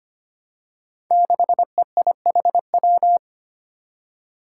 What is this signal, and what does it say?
Morse code, heard as a keyed tone, about 25 wpm, 700 Hz. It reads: 6EIHW
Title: Five Challenging Letter-Number Combinations 25wpm